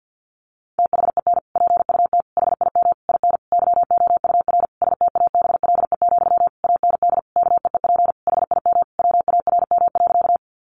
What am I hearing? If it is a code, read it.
Morse code, heard as a keyed tone, about 50 wpm, 700 Hz. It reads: THEN PUT HIM IN YOUR STABLE, AND KEEP HIM WARM.
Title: Sentences from Top 1500 Words 50wpm